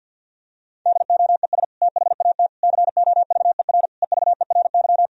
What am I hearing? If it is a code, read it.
Morse code, heard as a keyed tone, about 48 wpm, 700 Hz. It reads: DOES THAT COVER EVERY
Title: Sentences from Top 1000 Words - Review Part 1 - 50wpm